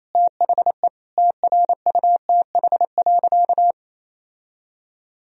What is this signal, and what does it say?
Morse code, heard as a keyed tone, about 28 wpm, 700 Hz. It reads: THE TRUTH.
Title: Sentences from Top 1000 Words - Review Part 1 - 28wpm